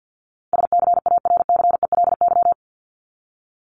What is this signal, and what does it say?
Morse code, heard as a keyed tone, about 50 wpm, 700 Hz. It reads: SCARCELY
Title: LY Suffix - Top 100 Words with Base Words 50wpm